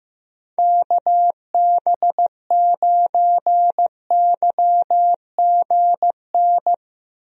Morse code, heard as a keyed tone, about 16 wpm, 700 Hz. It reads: KB9YGN
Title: USA Call Signs - 6 Characters - Repeated 3x - 15wpm